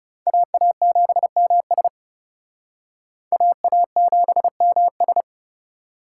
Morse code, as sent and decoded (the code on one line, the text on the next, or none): .- .- --... -- ... / .- .- --... -- ...
AA7MS AA7MS